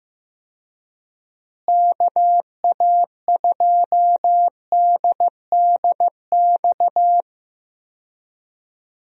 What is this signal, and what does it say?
Morse code, heard as a keyed tone, about 15 wpm, 700 Hz. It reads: KA2DDX
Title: USA Call Signs - 6 Characters - Repeated 2x - 15wpm